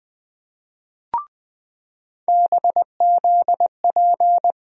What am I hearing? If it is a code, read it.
Morse code, heard as a keyed tone, about 20 wpm, 700 Hz. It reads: BZP
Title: USA Call Sign Postfixes - 3 Letters 20wpm